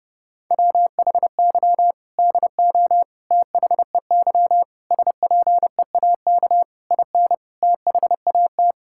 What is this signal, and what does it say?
Morse code, heard as a keyed tone, about 30 wpm, 700 Hz. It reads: WHY DO THEY SPEAK IN THAT
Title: Sentences from Top 1600 Words 30wpm